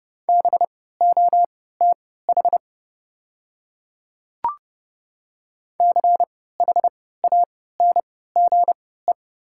Morse code, heard as a keyed tone, about 30 wpm, spaced as for 17 wpm, 700 Hz. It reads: BOTH CHANGE
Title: 3x Character Spacing - Top 300 Words - 30wpm